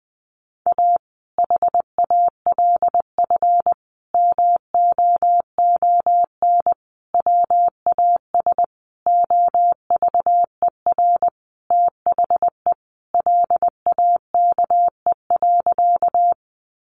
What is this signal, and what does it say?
Morse code, heard as a keyed tone, about 20 wpm, 700 Hz. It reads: A HALF MOON WAS OVER THE LAKE.
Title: Sentences from Top 1100 Words 20wpm